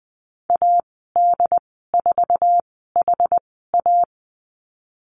Morse code, heard as a keyed tone, about 20 wpm, spaced as for 15 wpm, 700 Hz. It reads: AD4HA
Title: 088 Lesson 15g - Number 5 - Call Signs 20wpm